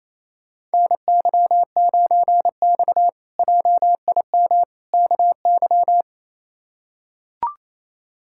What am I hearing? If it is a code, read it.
Morse code, heard as a keyed tone, about 28 wpm, 700 Hz. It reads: NY9X JIM KY
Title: K1USN SST QSO Practice - 28wpm